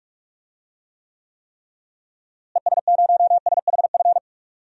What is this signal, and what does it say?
Morse code, heard as a keyed tone, about 46 wpm, 700 Hz. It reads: ES0SHF
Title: DXCC-100 Call Signs - 6 Characters - Repeated 3x - 45wpm